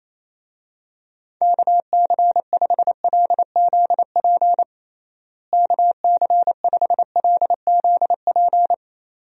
Morse code, heard as a keyed tone, about 28 wpm, 700 Hz. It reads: KC5LZP KC5LZP
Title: USA Call Signs - 6 Characters - Repeated 2x - 28wpm